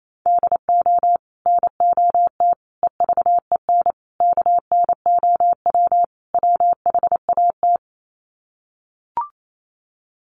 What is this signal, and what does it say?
Morse code, heard as a keyed tone, about 28 wpm, 700 Hz. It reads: DO NOT EVEN KNOW WHAT
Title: Sets of 5 Words Encore 28wpm